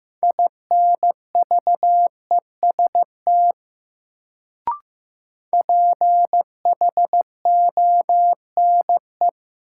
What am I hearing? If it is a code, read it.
Morse code, heard as a keyed tone, about 15 wpm, 700 Hz. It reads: INVEST PHONE
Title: Sentences from Top 1100 Words - New Word Drill - 15wpm